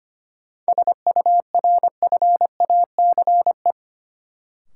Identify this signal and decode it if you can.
Morse code, heard as a keyed tone, about 25 wpm, 700 Hz. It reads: SURFACE
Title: Sentences from Top 600 Words - New Word Drill - 25wpm